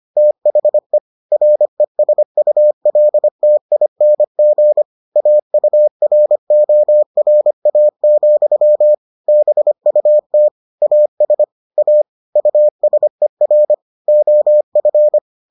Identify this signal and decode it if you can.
Morse code, heard as a keyed tone, about 25 wpm, 600 Hz. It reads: THE RESULTING AURORA, BUT AS A USER OF